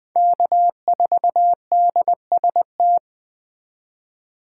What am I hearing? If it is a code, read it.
Morse code, heard as a keyed tone, about 20 wpm, 700 Hz. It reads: K4DST
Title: Speed Racing - USA Call Signs 17wpm